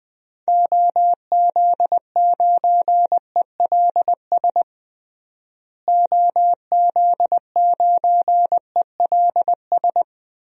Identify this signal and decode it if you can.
Morse code, heard as a keyed tone, about 20 wpm, 700 Hz. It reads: OZ9ELS OZ9ELS
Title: DXCC-100 Call Signs - 6 Characters - Repeated 2x - 20wpm